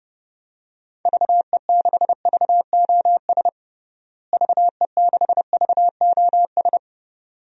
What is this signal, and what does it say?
Morse code, heard as a keyed tone, about 32 wpm, 700 Hz. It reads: VE6VOS VE6VOS
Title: DXCC-100 Call Signs - 6 Characters - Repeated 2x - 30wpm